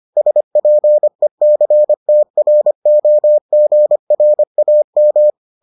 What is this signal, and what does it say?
Morse code, heard as a keyed tone, about 25 wpm, 600 Hz. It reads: SPECTROGRAM